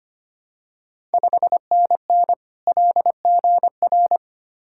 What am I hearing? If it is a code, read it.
Morse code, heard as a keyed tone, about 25 wpm, 700 Hz. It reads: 5NN LGR
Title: Ontario QSO Party Exchange 25wpm